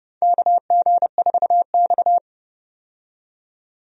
Morse code, heard as a keyed tone, about 30 wpm, 700 Hz. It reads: KG4X
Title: USA Call Signs - 4 Characters - 30wpm